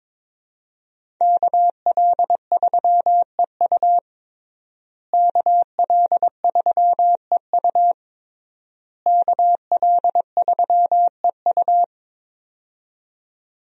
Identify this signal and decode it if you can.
Morse code, heard as a keyed tone, about 22 wpm, 700 Hz. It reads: KL3EU KL3EU KL3EU
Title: USA Call Signs - 5 Characters - Repeated 3x - 22wpm